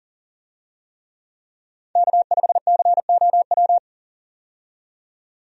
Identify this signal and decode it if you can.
Morse code, heard as a keyed tone, about 40 wpm, 700 Hz. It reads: K5COW
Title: USA Call Signs - 5 Characters - 40wpm